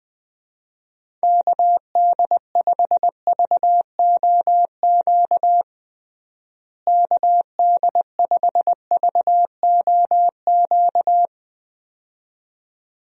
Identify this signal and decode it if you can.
Morse code, heard as a keyed tone, about 20 wpm, 700 Hz. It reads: KD5VOQ KD5VOQ
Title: USA Call Signs - 6 Characters - Repeated 2x - 20wpm